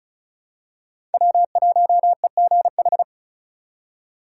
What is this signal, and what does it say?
Morse code, heard as a keyed tone, about 36 wpm, 700 Hz. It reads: W1EGH
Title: USA Call Signs - 5 Characters - 2x Character Spacing - 35wpm